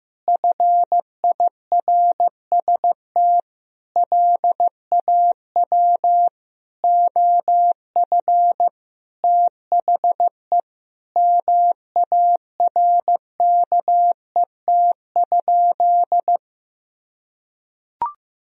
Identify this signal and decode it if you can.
Morse code, heard as a keyed tone, about 15 wpm, 700 Hz. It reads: FIRST LAW OF THE MARKET?
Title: Sentences from Top 1000 Words - Review Part 3 - 15wpm